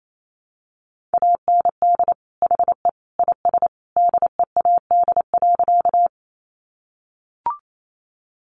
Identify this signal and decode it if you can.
Morse code, heard as a keyed tone, about 28 wpm, 700 Hz. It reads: AND HE IS DEAD.
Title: Sentences from Top 1500 Words 28wpm